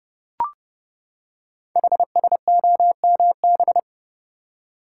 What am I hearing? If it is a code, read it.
Morse code, heard as a keyed tone, about 30 wpm, 700 Hz. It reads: HSOMB
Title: Code Groups - 5 Letters - 30wpm